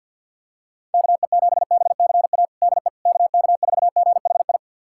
Morse code, heard as a keyed tone, about 51 wpm, 700 Hz. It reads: KE7BCA DE KK4GHI
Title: USA Callsign DE Callsign 50wpm